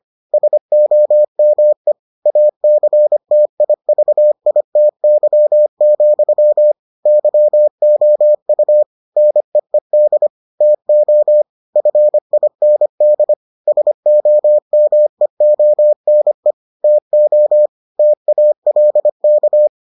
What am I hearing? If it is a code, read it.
Morse code, heard as a keyed tone, about 25 wpm, 600 Hz. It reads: SOME ACTIVITY, YOU NEED TO FIND SOMEONE TO TALK